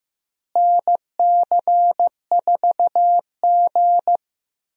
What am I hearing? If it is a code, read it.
Morse code, heard as a keyed tone, about 15 wpm, 700 Hz. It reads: NC4G
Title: USA Call Signs - 4 Characters - Repeated 2x - 15wpm